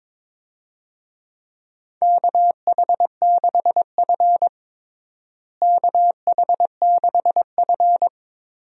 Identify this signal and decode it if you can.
Morse code, heard as a keyed tone, about 21 wpm, 700 Hz. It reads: KH6F KH6F
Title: DXCC-100 Call Signs - 4 Characters - Repeated 2x - 22wpm